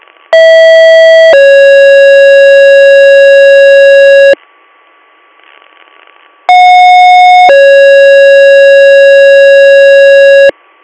Two-tone page, heard as a one-second tone, then a three-second tone, about 650 then 550 Hz, about 730 then 550 Hz, with noise. It is heard from 0.3 to 10.5 s.